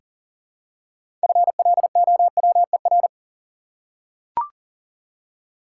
Morse code, heard as a keyed tone, about 40 wpm, 700 Hz. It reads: FLOWER
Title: Six Letter Words - 40wpm